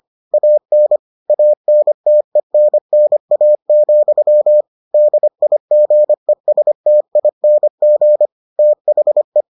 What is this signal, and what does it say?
Morse code, heard as a keyed tone, about 25 wpm, 600 Hz. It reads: AN ANTENNA, DIGESTING THE